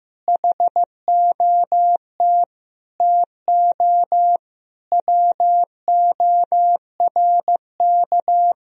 Morse code, heard as a keyed tone, about 15 wpm, 700 Hz. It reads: HOT TO WORK
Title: Sentences from Top 1000 Words - Review Part 4 - 15wpm